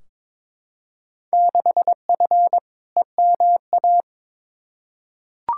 Morse code, heard as a keyed tone, about 22 wpm, 700 Hz. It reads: 6F EMA
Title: Field Day Exchanges Encore - 22wpm